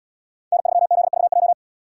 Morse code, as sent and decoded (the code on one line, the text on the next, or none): -. ...-- --.. ...- .---
N3ZVJ